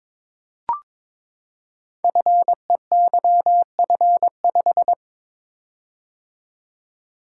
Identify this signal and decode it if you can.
Morse code, heard as a keyed tone, about 22 wpm, 700 Hz. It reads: FEYF5
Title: Code Groups - 5 Letter:Numbers - 22wpm